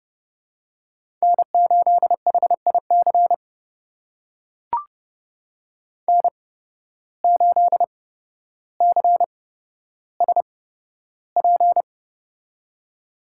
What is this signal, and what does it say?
Morse code, heard as a keyed tone, about 29 wpm, spaced as for 13 wpm, 700 Hz. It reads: N8HIC N8CSP
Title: USA Call Signs - 5 Characters - 8x Character Spacing - 30wpm